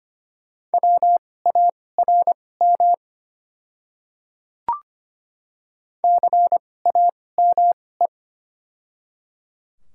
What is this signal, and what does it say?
Morse code, heard as a keyed tone, about 25 wpm, spaced as for 18 wpm, 700 Hz. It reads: WARM CAME
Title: 2x Character Spacing - Top 500 Words - 25wpm